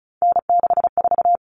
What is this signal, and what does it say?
Morse code, heard as a keyed tone, about 34 wpm, 700 Hz. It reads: N64
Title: Common Cut Numbers - 3-Digit 35wpm